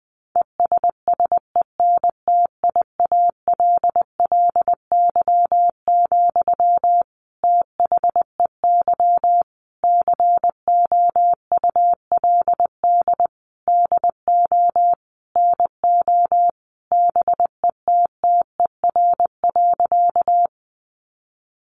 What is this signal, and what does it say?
Morse code, heard as a keyed tone, about 20 wpm, 700 Hz. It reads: ESSENTIALLY, THEY COULD DO NO BETTER.
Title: Sentences from Top 1300 Words 20wpm